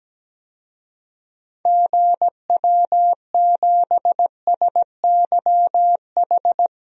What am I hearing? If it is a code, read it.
Morse code, heard as a keyed tone, about 17 wpm, 700 Hz. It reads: GW7SYH
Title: DXCC-100 Call Signs - 6 Characters - 17wpm